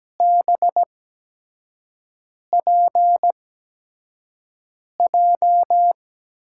Morse code, heard as a keyed tone, about 17 wpm, spaced as for 5 wpm, 700 Hz. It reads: BPJ
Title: USA Call Signs - 5 Characters - 8x Character Spacing - 17wpm